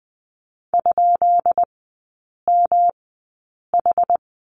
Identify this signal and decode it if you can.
Morse code, heard as a keyed tone, about 19 wpm, spaced as for 8 wpm, 700 Hz. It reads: ?MH
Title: 133 Lesson 22e - Question Mark - Three Characters 20wpm